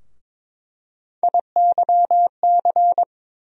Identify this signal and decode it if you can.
Morse code, heard as a keyed tone, about 22 wpm, 700 Hz. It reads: IYC
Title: USA Call Sign Postfixes - 3 Letters 22wpm